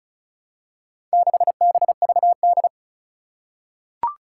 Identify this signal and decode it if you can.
Morse code, heard as a keyed tone, about 36 wpm, 700 Hz. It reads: 6BVD
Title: Four Challenging Letter-Number Combinations 35wpm